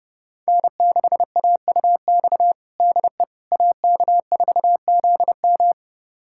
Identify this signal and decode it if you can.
Morse code, heard as a keyed tone, about 30 wpm, 700 Hz. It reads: N6AUX DE AK4ZM
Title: USA Callsign DE Callsign - 5 Characters - 30wpm